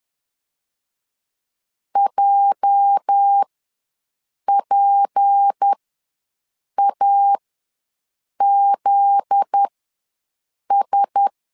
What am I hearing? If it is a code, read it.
Morse code, heard as a keyed tone, about 10 wpm, spaced as for 6 wpm, 800 Hz. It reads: JPAZS
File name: LEZIONI / 10 SIGNS / 10B INT ws 10wpm 4134